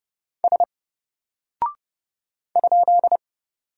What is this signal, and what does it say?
Morse code, heard as a keyed tone, about 30 wpm, 700 Hz. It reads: S ?
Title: Single Character (Letter:Number:Punctuation:Prosign) 30wpm